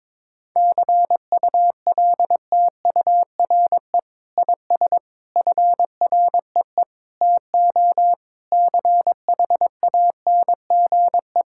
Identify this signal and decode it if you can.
Morse code, heard as a keyed tone, about 22 wpm, 700 Hz. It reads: CULTURE IS FREE TO CHANGE
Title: Sentences from Top 1000 Words - Review Part 6 - 22wpm